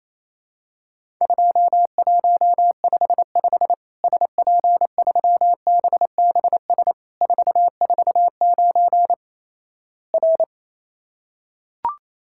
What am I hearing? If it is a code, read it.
Morse code, heard as a keyed tone, about 28 wpm, 700 Hz. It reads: 2155 SP3BBS 449 R
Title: Worked All Europe DX Contest - QTC Exchange - 28wpm